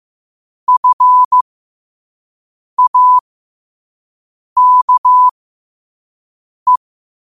Morse code, heard as a keyed tone, about 15 wpm, spaced as for 5 wpm, 1000 Hz. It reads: FAKE
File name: qotdfiles / 5wpm